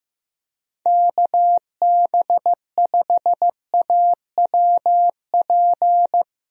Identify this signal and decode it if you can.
Morse code, heard as a keyed tone, about 15 wpm, 700 Hz. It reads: KB5AWP
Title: USA Call Signs - 6 Characters - Repeated 2x - 15wpm